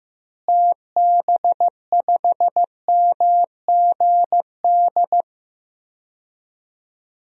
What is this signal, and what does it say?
Morse code, heard as a keyed tone, about 15 wpm, 700 Hz. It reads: TB5MGD
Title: DXCC-100 Call Signs - 6 Characters - Repeated 2x - 15wpm